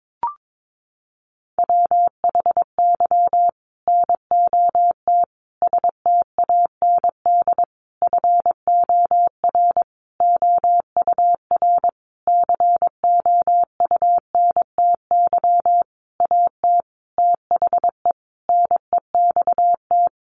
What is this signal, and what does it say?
Morse code, heard as a keyed tone, about 22 wpm, 700 Hz. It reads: WHY NOT STAND FOR OUR COUNTY AT THE NEXT
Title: Sentences from Top 1500 Words 22wpm